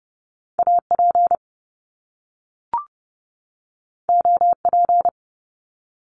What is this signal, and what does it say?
Morse code, heard as a keyed tone, about 30 wpm, 700 Hz. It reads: AP OP
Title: Top Two Letter Combinations in Words 30wpm